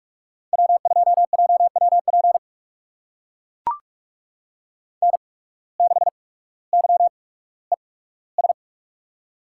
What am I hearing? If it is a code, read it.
Morse code, heard as a keyed tone, about 41 wpm, spaced as for 19 wpm, 700 Hz. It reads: W2JWP N6YES
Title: USA Call Signs - 5 Characters - 8x Character Spacing - 45wpm